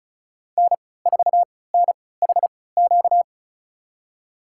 Morse code, heard as a keyed tone, about 36 wpm, spaced as for 20 wpm, 690 Hz. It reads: N4NHQ
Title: USA Call Signs - 5 Characters - 3x Character Spacing - 35wpm